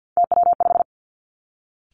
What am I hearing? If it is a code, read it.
Morse code, heard as a keyed tone, about 50 wpm, 700 Hz. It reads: TW5
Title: Three Challenging Letter-Number Combinations 50wpm